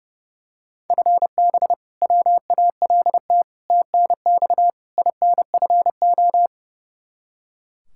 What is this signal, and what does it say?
Morse code, heard as a keyed tone, about 32 wpm, 700 Hz. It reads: FB WALT TNX INFO
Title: Practice QSO - Exchange 3 - More Info - 30wpm